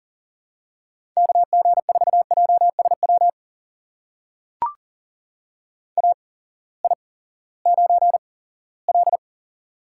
Morse code, heard as a keyed tone, about 40 wpm, spaced as for 20 wpm, 700 Hz. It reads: KG4JSW AI9L
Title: USA Call Signs - 8x Character Spacing - 40wpm